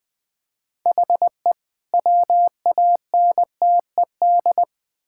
Morse code, heard as a keyed tone, about 20 wpm, 700 Hz. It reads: HE WANTED
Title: Build A Phrase - 3-Word Phrases - 20wpm